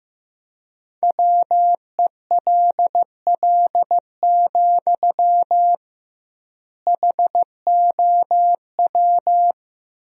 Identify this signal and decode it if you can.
Morse code, heard as a keyed tone, about 15 wpm, 700 Hz. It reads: WELL, HOW
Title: 2x Word-Spacing - Sentences from Top 200 Words - 15wpm